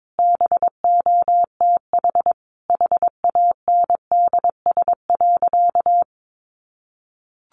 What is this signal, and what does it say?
Morse code, heard as a keyed tone, about 22 wpm, 700 Hz. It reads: BOTH HANDS.
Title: Easy Phrases 22wpm